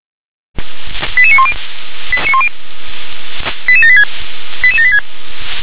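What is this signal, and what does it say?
Five-tone selective call (ZVEI), heard as quick runs of five tones, tones of about 70 ms. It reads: 88910 88786 88786
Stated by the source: ZVEI five-tone selective call received by SDR.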